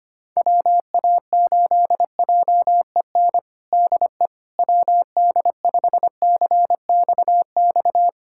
Morse code, heard as a keyed tone, about 25 wpm, 700 Hz. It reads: WA8JEN DE WD5CXX
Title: USA Callsign DE Callsign 25wpm